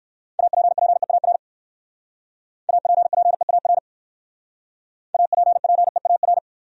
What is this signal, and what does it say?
Morse code, heard as a keyed tone, about 58 wpm, 700 Hz. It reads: APPEAR APPEAR APPEAR